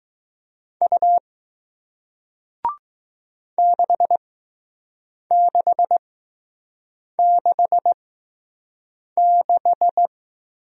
Morse code, heard as a keyed tone, about 20 wpm, spaced as for 6 wpm, 700 Hz. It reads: U 6666